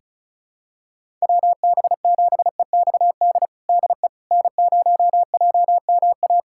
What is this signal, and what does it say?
Morse code, heard as a keyed tone, about 36 wpm, 700 Hz. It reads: WB7EXD DE N0JMA
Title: USA Callsign DE Callsign 35wpm